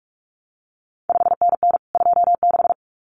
Morse code, heard as a keyed tone, about 46 wpm, 700 Hz. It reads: 5NN 26
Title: CQ WW DX Contest Exchange Encore - 45wpm